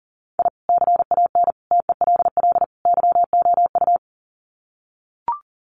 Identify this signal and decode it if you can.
Morse code, heard as a keyed tone, about 40 wpm, 700 Hz. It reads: I CAN TELL YOU
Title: Sets of 4 Words 40wpm